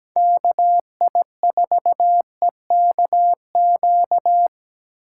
Morse code, heard as a keyed tone, about 17 wpm, 700 Hz. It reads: KI4EKQ
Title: USA Call Signs - 6 Characters - 8x Character Spacing - 17wpm